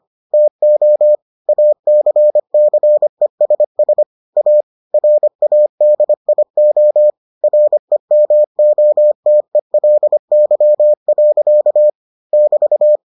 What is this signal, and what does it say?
Morse code, heard as a keyed tone, about 25 wpm, 600 Hz. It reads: TO ACCESS A RADIO REMOTELY. <BT>